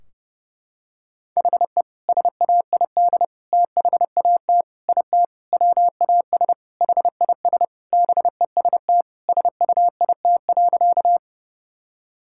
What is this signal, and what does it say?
Morse code, heard as a keyed tone, about 30 wpm, 700 Hz. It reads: HE SAID THAT IT WAS HIS BEST SUIT.
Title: Sentences from Top 1600 Words 30wpm